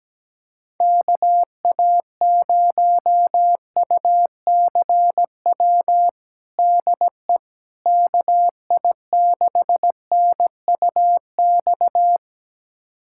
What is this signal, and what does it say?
Morse code, heard as a keyed tone, about 17 wpm, 700 Hz. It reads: KA0UCW DE KI6NUX
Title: USA Callsign DE Callsign - 6 Characters - 17wpm